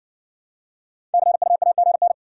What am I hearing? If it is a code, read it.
Morse code, heard as a keyed tone, about 55 wpm, 700 Hz. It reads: CUAGN